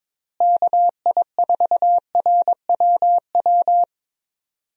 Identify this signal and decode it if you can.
Morse code, heard as a keyed tone, about 22 wpm, 700 Hz. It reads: KI4RWW
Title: USA Call Signs - 6 Characters - Repeated 2x - 22wpm